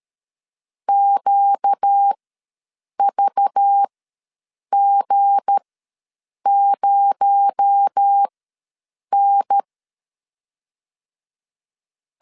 Morse code, heard as a keyed tone, about 13 wpm, spaced as for 7 wpm, 800 Hz. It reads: QVG0N
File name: LEZIONI / 7 GF9Z / 7 G ws 12wpm 3140